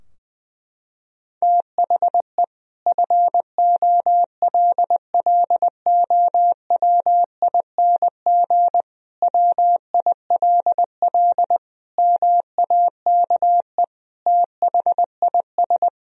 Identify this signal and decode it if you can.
Morse code, heard as a keyed tone, about 20 wpm, 700 Hz. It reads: THE FOLLOWING WILL MAKE THIS
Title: Sentences from Top 1000 Words - Review Part 4 - 20wpm